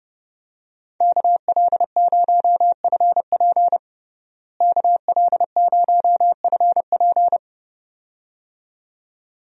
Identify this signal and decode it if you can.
Morse code, heard as a keyed tone, about 32 wpm, 700 Hz. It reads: KL0FP KL0FP
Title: USA Call Signs - 5 Characters - Repeated 2x - 30wpm